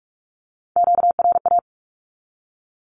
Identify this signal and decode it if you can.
Morse code, heard as a keyed tone, about 44 wpm, 700 Hz. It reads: QRA
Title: Q-Codes 45wpm